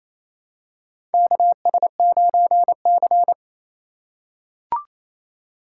Morse code, heard as a keyed tone, about 28 wpm, 700 Hz. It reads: KS9C
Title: USA Call Signs - 4 Characters - 3x Character Spacing - 28wpm